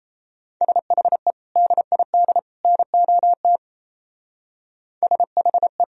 Morse code, heard as a keyed tone, about 34 wpm, 700 Hz. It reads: SHE DID NOT SHE
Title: Speed-Racing - Sets of 3 Words 28wpm